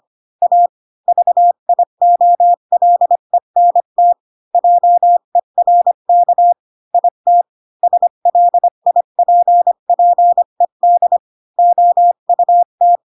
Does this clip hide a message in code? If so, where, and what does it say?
Yes: Morse code from 0.4 to 13.0 s, heard as a keyed tone, about 25 wpm, 700 Hz. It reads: A VIOLENT JERK IT SLIPPED OUT